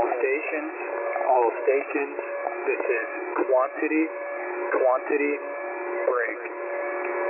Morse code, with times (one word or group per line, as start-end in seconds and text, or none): none